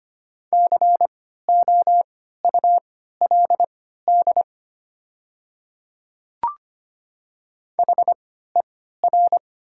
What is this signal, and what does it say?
Morse code, heard as a keyed tone, about 25 wpm, spaced as for 14 wpm, 700 Hz. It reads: COULD HER
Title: 3x Character Spacing - Top 100 Words - 25wpm